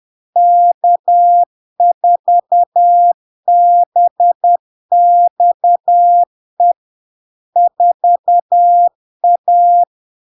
Morse code, heard as a keyed tone, about 10 wpm, 700 Hz. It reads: K4BXE 4A
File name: LARK-FD-2022-Log-in-CW-10wpm-700-Hz